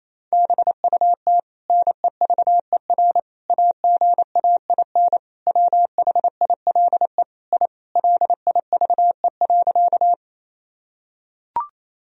Morse code, heard as a keyed tone, about 27 wpm, 700 Hz. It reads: BUT NEVER AGAIN WHILE I LIVE.
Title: Sentences from Top 1000 Words - Review Part 1 - 28wpm